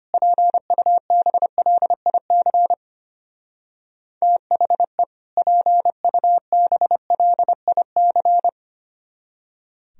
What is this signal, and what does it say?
Morse code, heard as a keyed tone, about 30 wpm, 700 Hz. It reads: PUBLIC THE PUBLIC
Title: Speed-Racing - Sets of 2 Words 25wpm